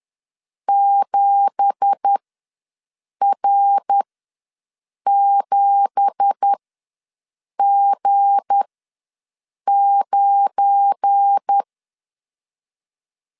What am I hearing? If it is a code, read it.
Morse code, heard as a keyed tone, about 10 wpm, spaced as for 6 wpm, 800 Hz. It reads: 7R7G9
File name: LEZIONI / 9 2867 ws 10wpm 4215